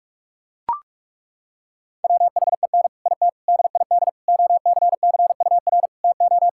Morse code, heard as a keyed tone, about 41 wpm, 700 Hz. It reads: WHEN IT DID OCCUR TO
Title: Sentences from Top 1000 Words - Review Part 6 - 45wpm